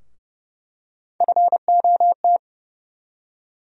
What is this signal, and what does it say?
Morse code, heard as a keyed tone, about 30 wpm, 700 Hz. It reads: FOT
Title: USA Call Sign Postfixes - 3 Letters 30wpm